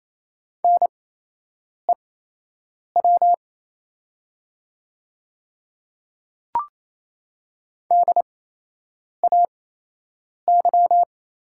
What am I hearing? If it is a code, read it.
Morse code, heard as a keyed tone, about 27 wpm, spaced as for 8 wpm, 700 Hz. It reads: NEW DAY
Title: 8x Character Spacing - Top 200 Words - 28wpm